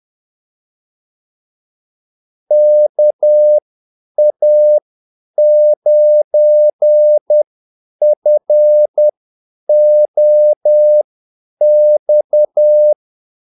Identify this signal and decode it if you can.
Morse code, heard as a keyed tone, about 10 wpm, spaced as for 8 wpm, 600 Hz. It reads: KA9FOX
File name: Callsigns-10-8wpm